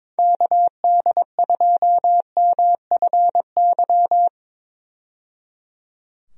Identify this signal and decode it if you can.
Morse code, heard as a keyed tone, about 22 wpm, 700 Hz. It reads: KD2MFY
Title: USA Call Signs - 6 Characters - 22wpm